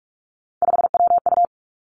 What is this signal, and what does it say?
Morse code, heard as a keyed tone, about 46 wpm, 700 Hz. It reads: 5WU
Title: Three Challenging Letter-Number Combinations 45wpm